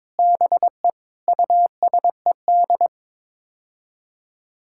Morse code, heard as a keyed tone, about 22 wpm, 700 Hz. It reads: BE USED
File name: Sets of 9 Words - 22wpm